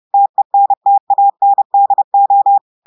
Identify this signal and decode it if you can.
Morse code, heard as a keyed tone, about 30 wpm, 800 Hz. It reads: TENTANDO
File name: audios / portugais / portugais texte en clair 1 vitesse 30 mots minutes